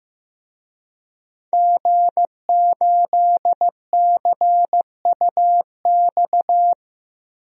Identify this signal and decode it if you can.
Morse code, heard as a keyed tone, about 15 wpm, 700 Hz. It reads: G8CUX
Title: DXCC-100 Call Signs - 5 Characters - Repeated 2x - 15wpm